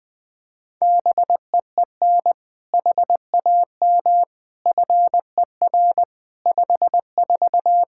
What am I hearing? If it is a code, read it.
Morse code, heard as a keyed tone, about 20 wpm, 700 Hz. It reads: BEEN HAM FER 54
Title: Practice QSO - Exchange 3 - More Info - 20wpm